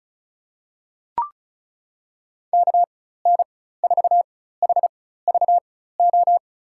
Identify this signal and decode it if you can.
Morse code, heard as a keyed tone, about 36 wpm, spaced as for 16 wpm, 700 Hz. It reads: KN4HVO
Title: USA Call Signs - 6 Characters - 4x Character Spacing - 35wpm